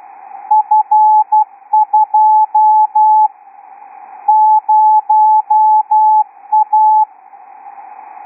Morse code, heard as a keyed tone, about 12 wpm, 850 Hz. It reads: F2 0A
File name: ITF2-XQUAD-AIRSPY-04